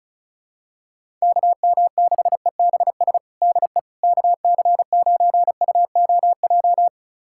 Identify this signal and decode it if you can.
Morse code, heard as a keyed tone, about 36 wpm, 700 Hz. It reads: KM6EBS DE KC9UOJ
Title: USA Callsign DE Callsign - 6 Characters - 35wpm